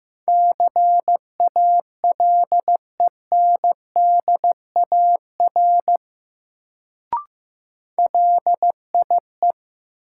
Morse code, heard as a keyed tone, about 15 wpm, 700 Hz. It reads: CALENDAR LIE
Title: Sentences from Top 1600 Words - New Word Drill Alt - 15wpm